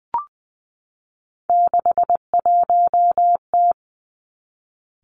Morse code, heard as a keyed tone, about 20 wpm, 700 Hz. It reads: 61T
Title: Three Challenging Letter-Number Combinations 20wpm